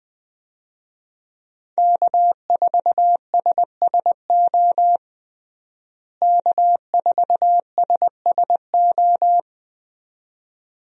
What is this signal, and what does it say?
Morse code, heard as a keyed tone, about 21 wpm, 700 Hz. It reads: K4SSO K4SSO
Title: USA Call Signs - 5 Characters - Repeated 2x - 20wpm